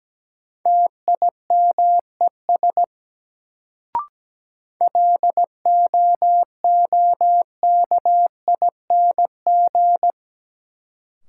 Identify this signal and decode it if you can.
Morse code, heard as a keyed tone, about 17 wpm, 700 Hz. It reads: TIMES LOOKING
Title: Sentences from Top 1000 Words - Review Part 6 - New Word Drill - 17wpm